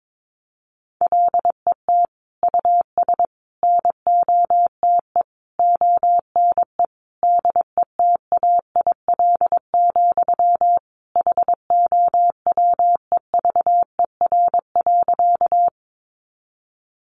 Morse code, heard as a keyed tone, about 22 wpm, 700 Hz. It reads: LET US NOTE ONE DETAIL, HOWEVER.
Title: Sentences from Top 1500 Words 22wpm